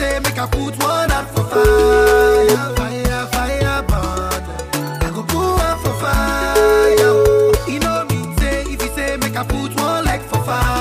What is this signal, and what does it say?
Ringback tone (Europe, 425 Hz), with noise.